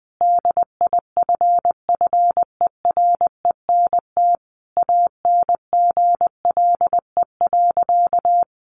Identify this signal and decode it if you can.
Morse code, heard as a keyed tone, about 19 wpm, 700 Hz. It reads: DIFFERENT ANGLE.
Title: Sentences from Top 1500 Words 20wpm